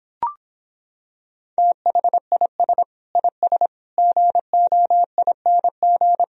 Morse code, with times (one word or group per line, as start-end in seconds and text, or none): 1.6-2.8 THIS
3.2-3.7 IS
4.0-6.2 GOING